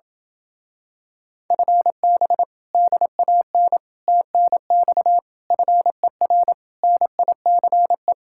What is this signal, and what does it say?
Morse code, heard as a keyed tone, about 28 wpm, 700 Hz. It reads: FB DAN TNX FER NICE